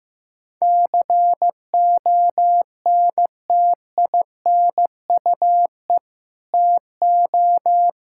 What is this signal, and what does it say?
Morse code, heard as a keyed tone, about 15 wpm, 700 Hz. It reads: CONTINUE TO
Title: Sentences from Top 1000 Words - Review Part 6 - 15wpm